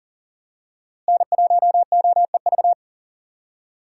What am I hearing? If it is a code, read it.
Morse code, heard as a keyed tone, about 40 wpm, 700 Hz. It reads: N1OEV
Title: USA Call Signs - 5 Characters - Repeated 3x - 40wpm